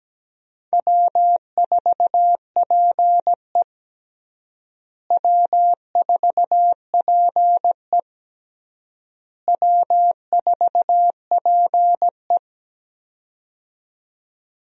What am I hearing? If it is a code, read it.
Morse code, heard as a keyed tone, about 17 wpm, 700 Hz. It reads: W4PE W4PE W4PE
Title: USA Call Signs - 4 Characters - Repeated 3x - 17wpm